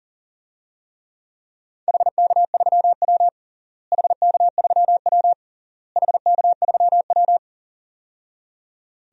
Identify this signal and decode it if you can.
Morse code, heard as a keyed tone, about 40 wpm, 700 Hz. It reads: HK3W HK3W HK3W
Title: DXCC-100 Call Signs - 4 Characters - Repeated 3x - 40wpm